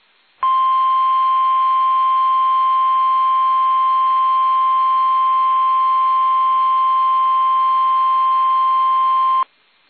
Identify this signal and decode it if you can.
Emergency Alert System (SAME) tones (the 1050 Hz attention tone). It is heard from 0.4 to 9.5 s.
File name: eas / KUSO / KUSO